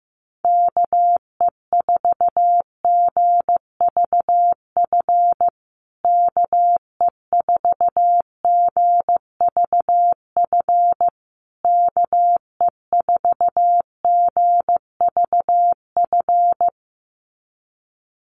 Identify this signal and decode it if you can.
Morse code, heard as a keyed tone, about 15 wpm, 700 Hz. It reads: KE4GVF KE4GVF KE4GVF
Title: USA Call Signs Repeated 3x 15wpm